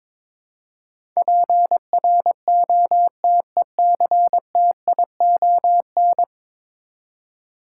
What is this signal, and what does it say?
Morse code, heard as a keyed tone, about 22 wpm, 700 Hz. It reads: PROTECTION
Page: morse-code-ninja-practice-22wpm